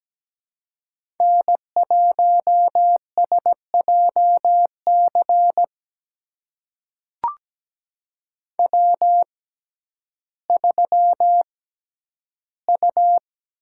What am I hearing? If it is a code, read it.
Morse code, heard as a keyed tone, about 17 wpm, spaced as for 10 wpm, 700 Hz. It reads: N1SJC W3U
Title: USA Call Signs - 5 Characters - 6x Character Spacing - 17wpm